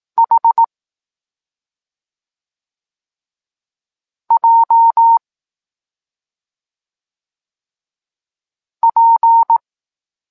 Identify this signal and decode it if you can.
Morse code, heard as a keyed tone, about 18 wpm, 920 Hz. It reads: H J P
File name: Tekst-26 1